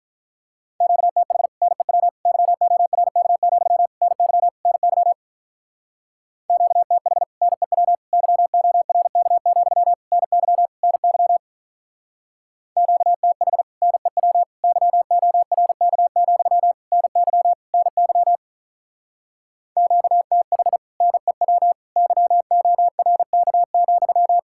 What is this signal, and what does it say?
Morse code, heard as a keyed tone, about 44 wpm, 700 Hz. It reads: QTH NEW YORK, NY NY QTH NEW YORK, NY NY QTH NEW YORK, NY NY QTH NEW YORK,